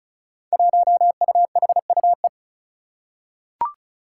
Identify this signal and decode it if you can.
Morse code, heard as a keyed tone, about 36 wpm, 700 Hz. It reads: 1UHUE
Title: Five Challenging Letter-Number Combinations 35wpm